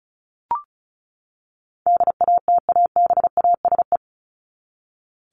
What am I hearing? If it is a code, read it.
Morse code, heard as a keyed tone, about 34 wpm, 700 Hz. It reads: DATABASE
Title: Top 1751-2000 Words 35wpm